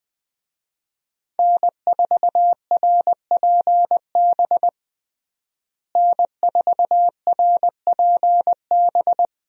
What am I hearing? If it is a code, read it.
Morse code, heard as a keyed tone, about 20 wpm, 700 Hz. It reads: N4RPB N4RPB
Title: USA Call Signs - 5 Characters - Repeated 3x - 20wpm